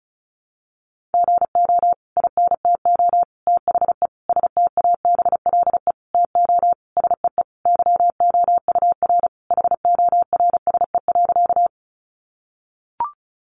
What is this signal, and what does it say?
Morse code, heard as a keyed tone, about 34 wpm, 700 Hz. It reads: GO INTO THE STABLE TO SEE YOUR HORSE.
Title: Sentences from Top 1500 Words 35wpm